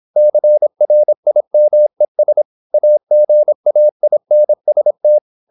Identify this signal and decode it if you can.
Morse code, heard as a keyed tone, about 26 wpm, 600 Hz. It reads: CRIMES AGAINST
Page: Never 26wpm | CW med Gnister